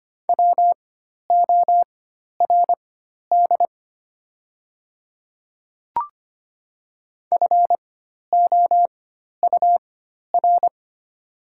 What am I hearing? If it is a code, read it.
Morse code, heard as a keyed tone, about 25 wpm, spaced as for 12 wpm, 700 Hz. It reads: WORD FOUR
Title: 4x Character Spacing - Top 300 Words - 25wpm